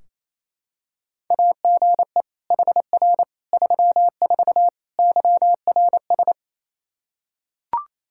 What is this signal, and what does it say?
Morse code, heard as a keyed tone, about 27 wpm, 700 Hz. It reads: AGE HR 34 YRS
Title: K1USN SST QSO Practice 28wpm